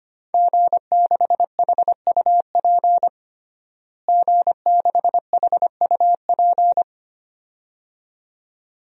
Morse code, heard as a keyed tone, about 25 wpm, 700 Hz. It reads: G6HUP G6HUP
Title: DXCC-100 Call Signs - 5 Characters - Repeated 2x - 25wpm